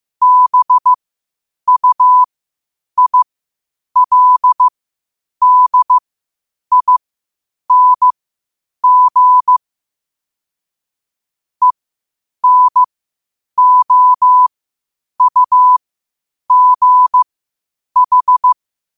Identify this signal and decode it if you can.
Morse code, heard as a keyed tone, about 15 wpm, spaced as for 8 wpm, 1000 Hz. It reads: BUILDING ENOUGH
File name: qotdfiles / 8wpm